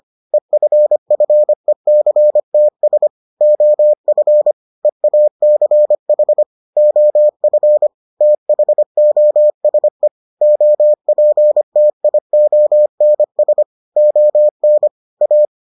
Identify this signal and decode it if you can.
Morse code, heard as a keyed tone, about 25 wpm, 600 Hz. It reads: EFFECTS OF EACH OF THOSE OPTIONS ON A